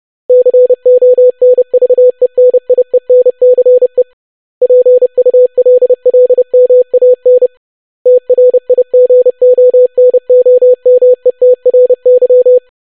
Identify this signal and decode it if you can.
Morse code, heard as a keyed tone, about 30 wpm, 500 Hz. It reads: CONVENIENCE PULLMAN TRIGONOMETRY
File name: longwords-25wpm